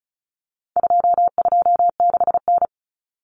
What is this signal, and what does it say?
Morse code, heard as a keyed tone, about 34 wpm, 700 Hz. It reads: 226N
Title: Common Cut Numbers - 4-Digit 35wpm